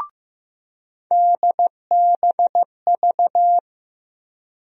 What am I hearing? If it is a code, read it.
Morse code, heard as a keyed tone, about 15 wpm, 700 Hz. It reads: DBV
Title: USA Call Sign Postfixes - 3 Letters 15wpm